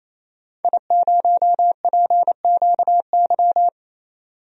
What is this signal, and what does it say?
Morse code, heard as a keyed tone, about 28 wpm, 700 Hz. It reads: I0PQY
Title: DXCC-100 Call Signs - 5 Characters - 28wpm